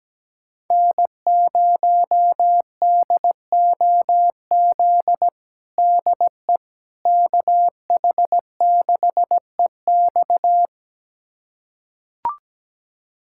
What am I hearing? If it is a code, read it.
Morse code, heard as a keyed tone, about 17 wpm, 700 Hz. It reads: N0DOZ DE KH6EX
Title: USA Callsign DE Callsign - 5 Characters - 17wpm